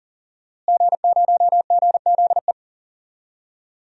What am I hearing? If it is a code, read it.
Morse code, heard as a keyed tone, about 40 wpm, 700 Hz. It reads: G0GZE
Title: DXCC-100 Call Signs - 5 Characters - Repeated 3x - 40wpm